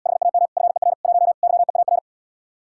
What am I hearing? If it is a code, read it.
Morse code, heard as a keyed tone, about 80 wpm, 700 Hz. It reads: HAM FER 1 YEAR